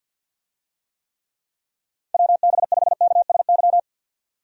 Morse code, heard as a keyed tone, about 46 wpm, 700 Hz. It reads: WB5KSY
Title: USA Call Signs - 6 Characters - 50wpm